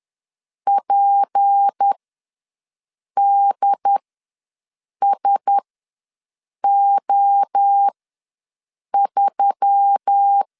Morse code, heard as a keyed tone, about 10 wpm, spaced as for 6 wpm, 800 Hz. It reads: PDSO3
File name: LEZIONI / 8 YJ45 / 8 YJ4 ws 10wpm 4020